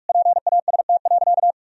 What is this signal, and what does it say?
Morse code, heard as a keyed tone, about 46 wpm, 700 Hz. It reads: PAST.
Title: Sentences from Top 1000 Words - Review Part 4 - 45wpm